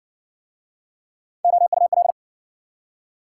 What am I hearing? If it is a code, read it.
Morse code, heard as a keyed tone, about 60 wpm, 700 Hz. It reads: OUL